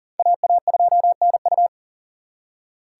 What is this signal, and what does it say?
Morse code, heard as a keyed tone, about 52 wpm, 700 Hz. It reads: AA2NU